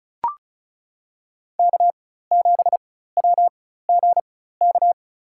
Morse code, heard as a keyed tone, about 36 wpm, spaced as for 16 wpm, 700 Hz. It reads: K7WGK
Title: USA Call Signs - 5 Characters - 4x Character Spacing - 35wpm